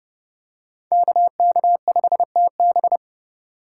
Morse code, heard as a keyed tone, about 29 wpm, 700 Hz. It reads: KK5TB